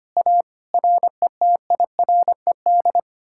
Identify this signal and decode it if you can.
Morse code, heard as a keyed tone, about 32 wpm, 700 Hz. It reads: A RETIRED